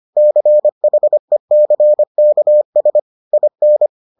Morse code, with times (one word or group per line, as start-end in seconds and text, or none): 0.2-3.0 CHECKS
3.3-3.9 IN